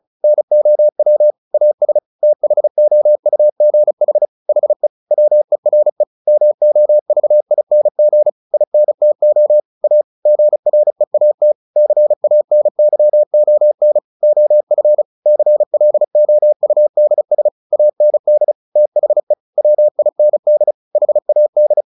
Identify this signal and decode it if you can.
Morse code, heard as a keyed tone, about 35 wpm, 600 Hz. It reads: NOW AS THOUGH HE WERE MOVING INTO A GREAT CANYON OF CLOUDS AND THE WIND HAD
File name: the old man and the sea 2 35wpm